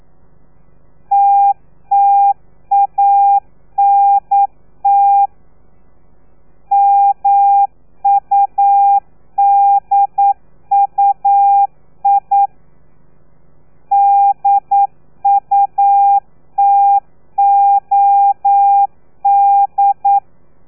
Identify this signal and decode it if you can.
Morse code, heard as a keyed tone, about 9 wpm, 800 Hz. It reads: TTANT MUDUI DUTOD
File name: telegrafia / i2tfj / Lez3 40